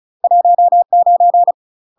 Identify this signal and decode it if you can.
Morse code, heard as a keyed tone, about 34 wpm, 700 Hz. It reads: 19
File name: UK MASTER QSOs TRAINING / 35WPM / UK MASTER QSOs 0021